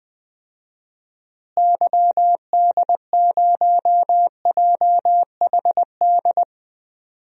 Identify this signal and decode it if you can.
Morse code, heard as a keyed tone, about 20 wpm, 700 Hz. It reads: YD0JHD
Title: DXCC-100 Call Signs - 6 Characters - Repeated 2x - 20wpm